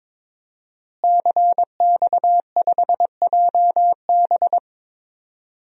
Morse code, heard as a keyed tone, about 22 wpm, 700 Hz. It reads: CX5JB
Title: DXCC-100 Call Signs - 5 Characters - Repeated 3x - 22wpm